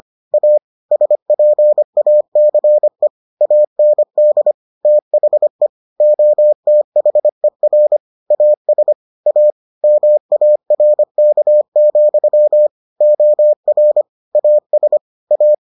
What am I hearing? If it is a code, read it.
Morse code, heard as a keyed tone, about 25 wpm, 600 Hz. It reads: A SPACE AND THE OTHER AS A MARK, OR AS A